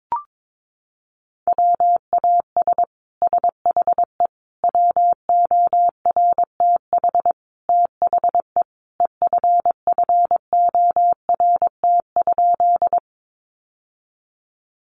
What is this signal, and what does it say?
Morse code, heard as a keyed tone, about 22 wpm, 700 Hz. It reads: WAS SHE WORTH THE EFFORT?
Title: Sentences from Top 1200 Words 22wpm